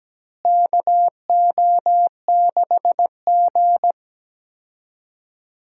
Morse code, heard as a keyed tone, about 17 wpm, 700 Hz. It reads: KO6G
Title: USA Call Signs - 4 Characters - Repeated 3x - 17wpm